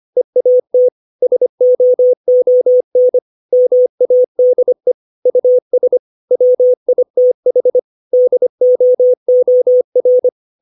Morse code, heard as a keyed tone, about 25 wpm, 500 Hz. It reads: EAT SOON MADE US WITH DOOR
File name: 632 words / 632 Words 25WPM